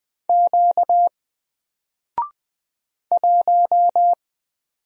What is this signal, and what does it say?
Morse code, heard as a keyed tone, about 20 wpm, 700 Hz. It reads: Q 1
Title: Single Character (Letter:Number:Punctuation:Prosign) Alt 20wpm